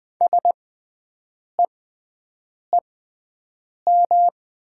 Morse code, heard as a keyed tone, about 20 wpm, spaced as for 7 wpm, 700 Hz. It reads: SEEM
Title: 6x Character Spacing - Top 300 Words - 20wpm